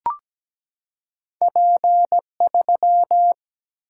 Morse code, heard as a keyed tone, about 17 wpm, 700 Hz. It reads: P3
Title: Single-Letter Single-Number 17wpm